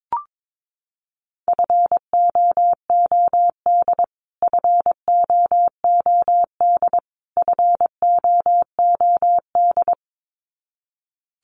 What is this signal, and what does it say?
Morse code, heard as a keyed tone, about 22 wpm, 700 Hz. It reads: FOOD FOOD FOOD
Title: Top 500 Words Repeated 22wpm